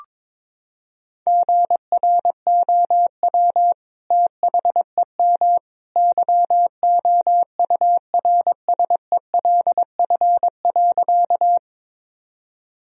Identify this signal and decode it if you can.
Morse code, heard as a keyed tone, about 22 wpm, 700 Hz. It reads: GROW THEM YOURSELF.
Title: Sentences from Top 1000 Words - Review Part 1 - 22wpm